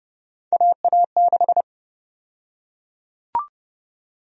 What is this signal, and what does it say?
Morse code, heard as a keyed tone, about 32 wpm, 700 Hz. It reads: AA6
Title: USA Call Sign Prefixes - Repeated 2x - 30wpm